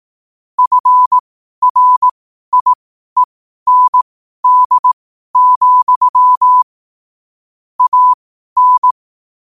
Morse code, heard as a keyed tone, about 18 wpm, spaced as for 12 wpm, 1000 Hz. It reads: FRIEND, AN
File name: qotdfiles / 12wpm